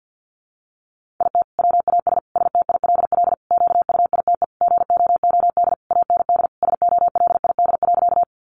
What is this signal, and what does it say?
Morse code, heard as a keyed tone, about 50 wpm, 700 Hz. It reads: IT WAS STILL QUITE GOOD AND SOLID.
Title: Sentences from Top 1500 Words 50wpm